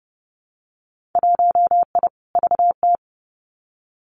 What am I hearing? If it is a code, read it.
Morse code, heard as a keyed tone, about 30 wpm, 700 Hz. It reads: 1I VT
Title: Winter Field Day Exchanges Encore 30wpm